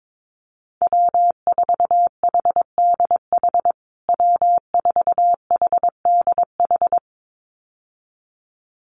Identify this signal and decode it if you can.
Morse code, heard as a keyed tone, about 22 wpm, 700 Hz. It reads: W4HDH W4HDH
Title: USA Call Signs Repeated 2x 22wpm